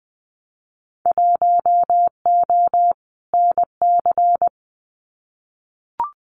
Morse code, heard as a keyed tone, about 20 wpm, 700 Hz. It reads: 1O NC
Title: Winter Field Day Exchanges 20wpm